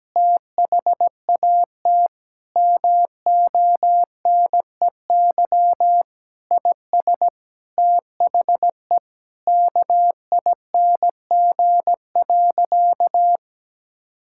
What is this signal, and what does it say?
Morse code, heard as a keyed tone, about 17 wpm, 700 Hz. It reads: THAT MONEY IS THE KING.
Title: Sentences from Top 1600 Words 17wpm